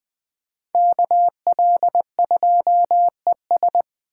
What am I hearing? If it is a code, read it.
Morse code, heard as a keyed tone, about 20 wpm, 700 Hz. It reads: KL2ES
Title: Speed Racing - USA 2x2 Call Signs 17wpm